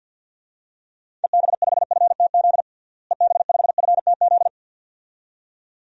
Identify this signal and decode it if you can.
Morse code, heard as a keyed tone, about 50 wpm, 700 Hz. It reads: EB5FTZ EB5FTZ
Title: DXCC-100 Call Signs - 6 Characters - Repeated 2x - 50wpm